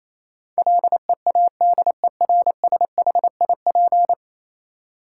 Morse code, heard as a keyed tone, about 28 wpm, 700 Hz. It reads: LEADERSHIP
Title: Sentences from Top 1600 Words - New Word Drill - 28wpm